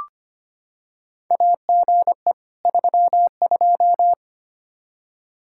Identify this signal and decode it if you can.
Morse code, heard as a keyed tone, about 25 wpm, 700 Hz. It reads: AGE 32
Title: Practice QSO - Exchange 3 - More Info - 25wpm